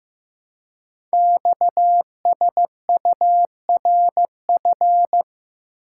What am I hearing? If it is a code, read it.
Morse code, heard as a keyed tone, about 15 wpm, 700 Hz. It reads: XSURF
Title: Code Groups - 5 Letters - 15wpm